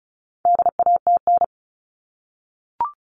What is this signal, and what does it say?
Morse code, heard as a keyed tone, about 34 wpm, 700 Hz. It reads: DATN
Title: All Cut Numbers - 4-Digit 35wpm